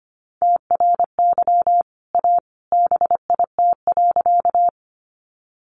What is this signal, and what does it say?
Morse code, heard as a keyed tone, about 25 wpm, 700 Hz. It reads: TRY A BIT.
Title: Sentences from Top 900 Words 25wpm